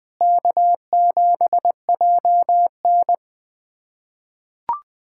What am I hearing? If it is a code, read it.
Morse code, heard as a keyed tone, about 20 wpm, 700 Hz. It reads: K7JN
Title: USA Callsign DE Callsign - 4 Characters - 20wpm